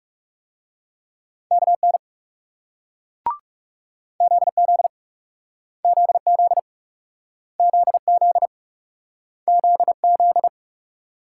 Morse code, heard as a keyed tone, about 39 wpm, 700 Hz. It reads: KN ZZ ZZ ZZ ZZ